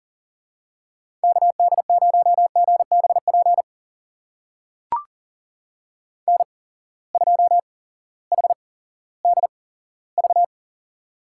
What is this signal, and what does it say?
Morse code, heard as a keyed tone, about 42 wpm, spaced as for 19 wpm, 700 Hz. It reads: KD0GBP N2HDV
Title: USA Call Signs - 8x Character Spacing - 40wpm